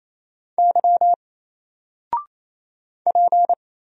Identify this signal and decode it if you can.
Morse code, heard as a keyed tone, about 27 wpm, 700 Hz. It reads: Y P
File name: Single-Letter-Number-28wpm